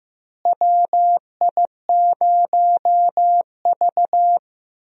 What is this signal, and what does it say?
Morse code, heard as a keyed tone, about 15 wpm, 700 Hz. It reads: WI0V
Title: USA Call Signs - 4 Characters - Repeated 2x - 15wpm